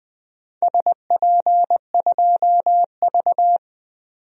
Morse code, heard as a keyed tone, about 20 wpm, 700 Hz. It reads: SP2V
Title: DXCC-100 Call Signs - 4 Characters - 20wpm